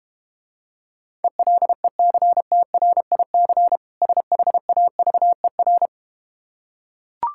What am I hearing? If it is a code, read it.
Morse code, heard as a keyed tone, about 32 wpm, 700 Hz. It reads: ELECTRIC SHAVER